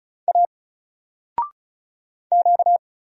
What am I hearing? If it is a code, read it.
Morse code, heard as a keyed tone, about 34 wpm, 700 Hz. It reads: A Q
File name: Single-Letter-Number-35wpm